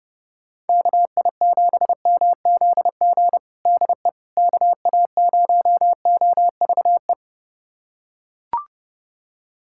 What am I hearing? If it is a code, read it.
Morse code, heard as a keyed tone, about 30 wpm, 700 Hz. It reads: KI7MZG DE KA0OVE
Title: USA Callsign DE Callsign 30wpm